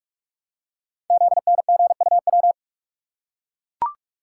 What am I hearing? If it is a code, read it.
Morse code, heard as a keyed tone, about 44 wpm, 690 Hz. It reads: ZNGUW
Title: Code Groups - 5 Letters - 45wpm